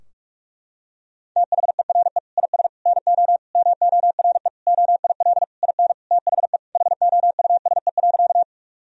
Morse code, heard as a keyed tone, about 46 wpm, 700 Hz. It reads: THERE IS NO MORE OIL IN THE HOUSE.
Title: Sentences from Top 1000 Words - Review Part 4 - 45wpm